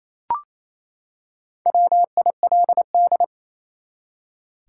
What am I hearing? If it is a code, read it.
Morse code, heard as a keyed tone, about 27 wpm, 700 Hz. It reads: WILD
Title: Four Letter Words - 28wpm